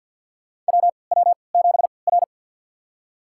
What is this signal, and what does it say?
Morse code, heard as a keyed tone, about 53 wpm, spaced as for 29 wpm, 700 Hz. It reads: WW7R
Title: USA Call Signs - 4 Characters - 3x Character Spacing - 50wpm